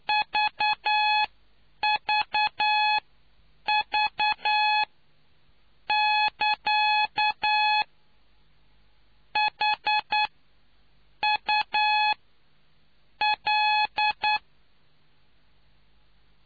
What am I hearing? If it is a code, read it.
Morse code, heard as a keyed tone, about 9 wpm, spaced as for 7 wpm, 850 Hz. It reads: VVV <KA> H U L